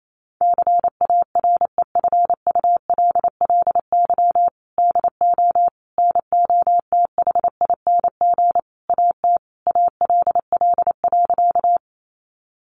Morse code, heard as a keyed tone, about 28 wpm, 700 Hz. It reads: CAREFULLY DO NOTHING AT ALL.
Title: Sentences from Top 1300 Words 28wpm